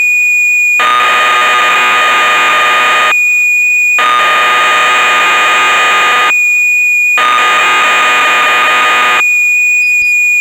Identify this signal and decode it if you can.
Cassette data (Kansas City type).